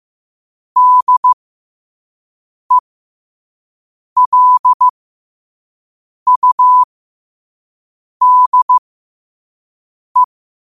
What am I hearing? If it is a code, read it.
Morse code, heard as a keyed tone, about 15 wpm, spaced as for 5 wpm, 1000 Hz. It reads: DELUDE
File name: qotdfiles / 5wpm